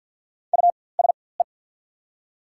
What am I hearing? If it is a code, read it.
Morse code, heard as a keyed tone, about 48 wpm, spaced as for 23 wpm, 690 Hz. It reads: USE
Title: 4x Character Spacing - Top 100 Words - 50wpm